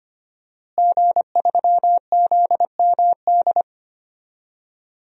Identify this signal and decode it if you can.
Morse code, heard as a keyed tone, about 25 wpm, 700 Hz. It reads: G3ZMD
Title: DXCC-100 Call Signs - 5 Characters - Repeated 2x - 25wpm